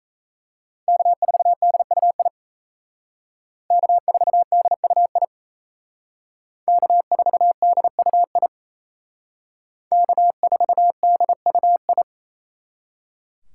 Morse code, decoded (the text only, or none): K4DUI K4DUI K4DUI K4DUI